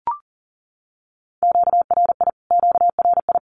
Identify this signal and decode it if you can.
Morse code, heard as a keyed tone, about 37 wpm, 700 Hz. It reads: QRI QRI
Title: Q-Codes Repeated 3x 40wpm